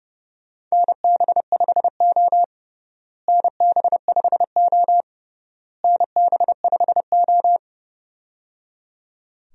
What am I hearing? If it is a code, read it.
Morse code, heard as a keyed tone, about 30 wpm, 700 Hz. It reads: NB5O NB5O NB5O
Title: USA Call Signs - 4 Characters - Repeated 3x - 30wpm